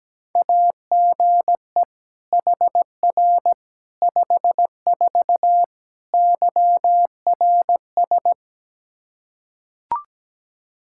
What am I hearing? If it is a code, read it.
Morse code, heard as a keyed tone, about 17 wpm, 700 Hz. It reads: AGE HR 54 YRS
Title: Practice QSO - Exchange 3 - More Info - 17wpm